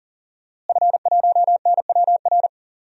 Morse code, heard as a keyed tone, about 40 wpm, 700 Hz. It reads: F1NWR
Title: DXCC-100 Call Signs - 5 Characters - Repeated 2x - 40wpm